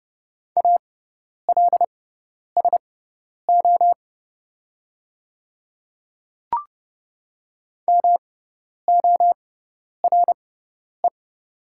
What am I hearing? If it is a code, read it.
Morse code, heard as a keyed tone, about 30 wpm, spaced as for 10 wpm, 700 Hz. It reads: ALSO MORE
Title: 6x Character Spacing - Top 200 Words - 30wpm